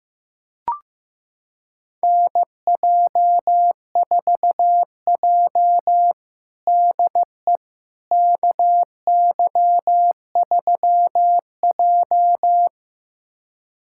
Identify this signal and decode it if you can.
Morse code, heard as a keyed tone, about 16 wpm, 700 Hz. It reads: NJ4J DE KY3J
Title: USA Callsign DE Callsign - 4 Characters - 15wpm